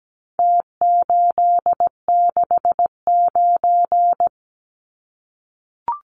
Morse code, heard as a keyed tone, about 17 wpm, 700 Hz. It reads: T869
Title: Numbers - Contest Style - 4-Digit 17wpm